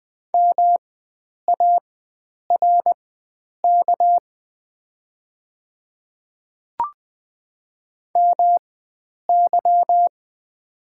Morse code, heard as a keyed tone, about 20 wpm, spaced as for 9 wpm, 700 Hz. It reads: MARK MY
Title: 4x Character Spacing - Top 500 Words - 20wpm